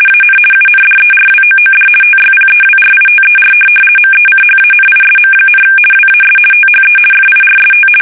RTTY, about 50 baud. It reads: AN UNDERWOOD TYPEWRITER MECHANISM, WAS THE FIRST
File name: fsk850-50B / teletype-corp